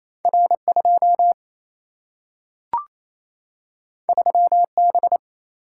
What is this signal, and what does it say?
Morse code, heard as a keyed tone, about 28 wpm, 700 Hz. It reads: R 2 3 B
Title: Single-Letter Single-Number 28wpm